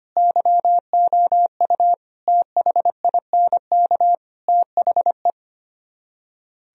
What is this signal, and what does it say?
Morse code, heard as a keyed tone, about 33 wpm, 700 Hz. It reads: YOU THINK THE